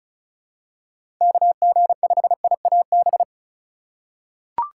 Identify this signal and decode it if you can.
Morse code, heard as a keyed tone, about 36 wpm, 700 Hz. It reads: KG5IAB
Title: USA Call Signs - 4x Character Spacing - 35wpm